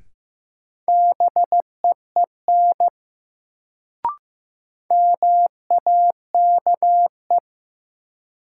Morse code, heard as a keyed tone, about 15 wpm, 700 Hz. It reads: BEEN MAKE
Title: Sets of 2 Words - New Word Drill - 15wpm